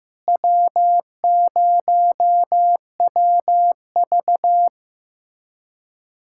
Morse code, heard as a keyed tone, about 16 wpm, 700 Hz. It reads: W0WV
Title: USA Callsign DE Callsign - 4 Characters - 15wpm